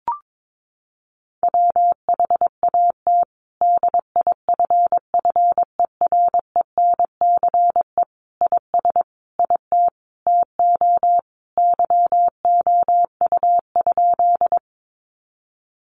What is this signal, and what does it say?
Morse code, heard as a keyed tone, about 22 wpm, 700 Hz. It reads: WHAT DIFFERENCE IS IT TO YOU?
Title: Sentences from Top 800 Words 22wpm